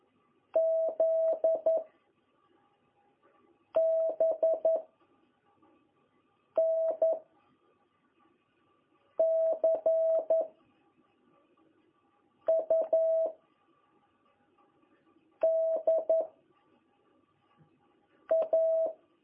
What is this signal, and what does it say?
Morse code, heard as a keyed tone, about 11 wpm, 660 Hz. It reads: Z B N C U D A
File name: jh8cbh / 2021cw